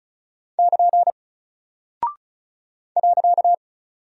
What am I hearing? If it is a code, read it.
Morse code, heard as a keyed tone, about 34 wpm, 700 Hz. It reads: <KN> .
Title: Single Character (Letter:Number:Punctuation:Prosign) 35wpm